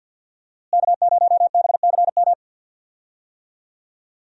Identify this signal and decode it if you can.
Morse code, heard as a keyed tone, about 52 wpm, 700 Hz. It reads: K0BCM
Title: USA Call Signs - 5 Characters - Repeated 2x - 50wpm